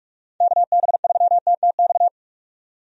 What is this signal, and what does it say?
Morse code, heard as a keyed tone, about 46 wpm, 700 Hz. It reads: KB3TTX
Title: USA Call Signs - 6 Characters - 45wpm